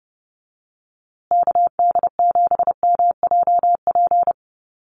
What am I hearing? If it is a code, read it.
Morse code, heard as a keyed tone, about 30 wpm, 700 Hz. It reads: KD7MJP
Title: USA Call Signs Repeated 3x 30wpm